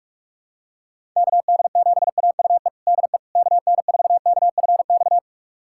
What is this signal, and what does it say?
Morse code, heard as a keyed tone, about 46 wpm, 700 Hz. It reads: KD7AUE DE KN4KFX
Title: USA Callsign DE Callsign - 6 Characters - 45wpm